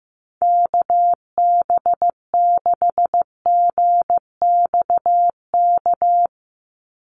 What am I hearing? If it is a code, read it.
Morse code, heard as a keyed tone, about 15 wpm, 700 Hz. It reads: KB6GXK
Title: USA Call Signs 15wpm